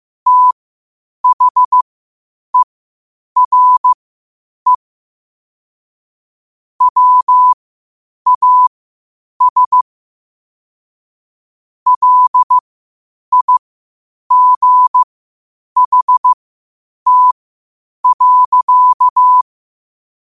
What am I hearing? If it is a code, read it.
Morse code, heard as a keyed tone, about 15 wpm, spaced as for 8 wpm, 1000 Hz. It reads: THERE WAS LIGHT.
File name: 8wpm